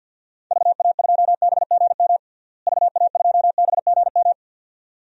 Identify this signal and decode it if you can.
Morse code, heard as a keyed tone, about 50 wpm, 700 Hz. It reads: VA2BGM VA2BGM
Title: DXCC-100 Call Signs - 6 Characters - Repeated 2x - 50wpm